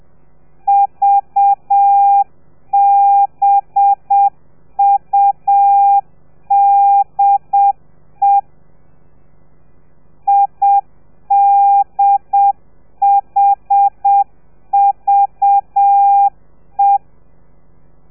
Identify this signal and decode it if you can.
Morse code, heard as a keyed tone, about 7 wpm, 800 Hz. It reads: VBUDE IDHVE
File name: telegrafia / i2tfj / Lez4 30